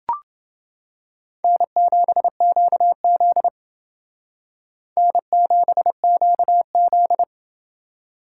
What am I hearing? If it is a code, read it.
Morse code, heard as a keyed tone, about 28 wpm, 700 Hz. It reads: N7QZ N7QZ